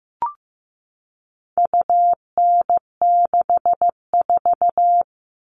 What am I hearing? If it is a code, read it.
Morse code, heard as a keyed tone, about 15 wpm, 700 Hz. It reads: UN64
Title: All Cut Numbers - 4-Digit 15wpm